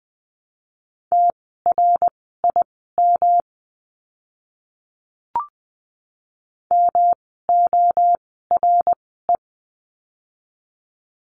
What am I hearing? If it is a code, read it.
Morse code, heard as a keyed tone, about 20 wpm, spaced as for 15 wpm, 700 Hz. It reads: TRIM MORE
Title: 108 Lesson 18f - Letter m - Words 20wpm